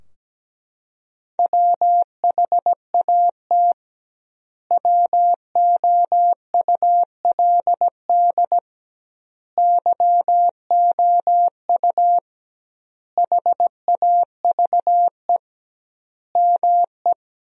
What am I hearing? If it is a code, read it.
Morse code, heard as a keyed tone, about 17 wpm, 700 Hz. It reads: WHAT WOULD YOU HAVE ME
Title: 2x Word-Spacing - Sentences from Top 300 Words - 17wpm